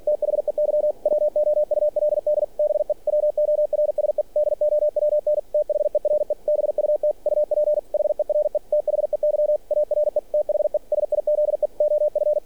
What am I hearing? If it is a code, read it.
Morse code, heard as a keyed tone, about 46 wpm, 600 Hz. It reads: THEY WOULD BE WORSE DOWN THERE BUT UP HERE THEY ARE THE SIZE OF
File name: archived / CRSnet